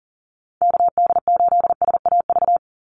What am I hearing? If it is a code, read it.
Morse code, heard as a keyed tone, about 40 wpm, 700 Hz. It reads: KD8SAV
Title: Call Signs - 40wpm